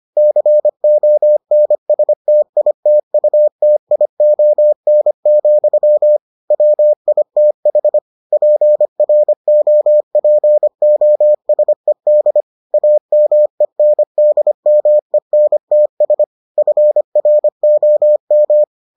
Morse code, heard as a keyed tone, about 25 wpm, 600 Hz. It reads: CONSTITUTION, WITH PROPOSED AMENDMENTS FROM